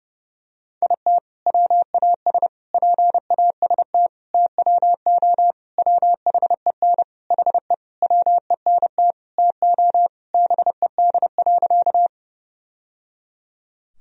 Morse code, heard as a keyed tone, about 32 wpm, 700 Hz. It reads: IT WAS PAST TWO WHEN HE WENT TO BED.
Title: Sentences from Top 1000 Words - Review Part 3 - 30wpm